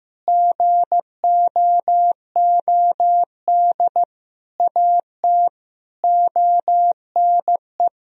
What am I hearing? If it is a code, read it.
Morse code, heard as a keyed tone, about 15 wpm, 700 Hz. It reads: GOOD AT ONE
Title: Sentences from Top 1000 Words - Review Part 2 - 15wpm